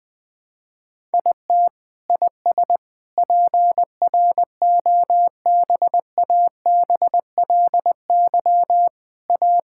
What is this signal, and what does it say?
Morse code, heard as a keyed tone, about 20 wpm, 700 Hz. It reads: IT IS PROBABLY A
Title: Sentences from Top 1000 Words - Review Part 6 - 20wpm